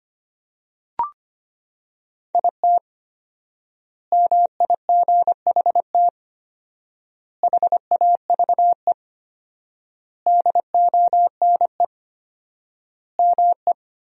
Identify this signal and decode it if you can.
Morse code, heard as a keyed tone, about 25 wpm, 700 Hz. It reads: IT MIGHT HAVE DONE ME
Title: 4x Word-Spacing - Sentences from Top 500 Words - 25wpm